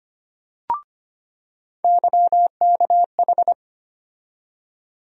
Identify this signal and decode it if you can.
Morse code, heard as a keyed tone, about 25 wpm, 700 Hz. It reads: YKH
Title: USA Call Sign Postfixes - 3 Letters 25wpm